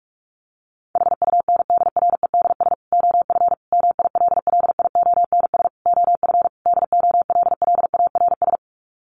Morse code, heard as a keyed tone, about 44 wpm, 700 Hz. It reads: HUNDREDS OF MILLIONS OF DOLLARS
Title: Sets of 5 Words 45wpm